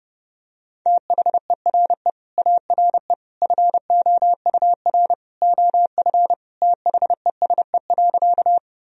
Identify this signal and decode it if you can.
Morse code, heard as a keyed tone, about 32 wpm, 700 Hz. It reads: THERE ARE FOUR OF THESE.
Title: Sentences from Top 1000 Words - Review Part 1 - 30wpm